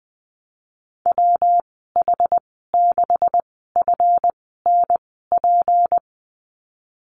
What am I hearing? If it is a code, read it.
Morse code, heard as a keyed tone, about 20 wpm, spaced as for 14 wpm, 700 Hz. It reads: WH6FNP
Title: 186 Lesson 30g - Letter b - Call Signs 20wpm